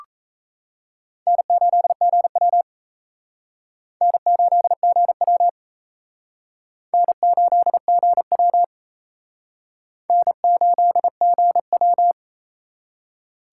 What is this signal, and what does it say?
Morse code, heard as a keyed tone, about 34 wpm, 700 Hz. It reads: N8GW N8GW N8GW N8GW
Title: Speed Racing - USA 1x2 Call Signs 28wpm